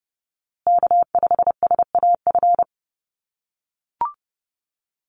Morse code, heard as a keyed tone, about 30 wpm, 700 Hz. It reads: K5SAF
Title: USA Call Signs Repeated 2x 30wpm